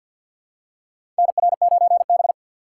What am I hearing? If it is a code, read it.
Morse code, heard as a keyed tone, about 53 wpm, 700 Hz. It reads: NR9B
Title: USA Call Signs - 4 Characters - 2x Character Spacing - 50wpm